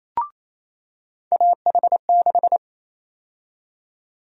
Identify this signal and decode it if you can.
Morse code, heard as a keyed tone, about 28 wpm, 700 Hz. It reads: AH6
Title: USA Call Sign Prefixes - 28wpm